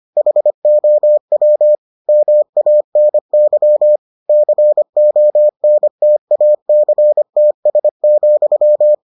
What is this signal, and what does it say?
Morse code, heard as a keyed tone, about 25 wpm, 600 Hz. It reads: HOW MANY CONTACTS,